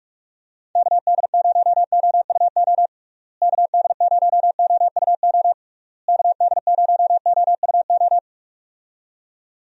Morse code, heard as a keyed tone, about 46 wpm, 700 Hz. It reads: KD0OUO KD0OUO KD0OUO
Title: USA Call Signs - 6 Characters - Repeated 3x - 45wpm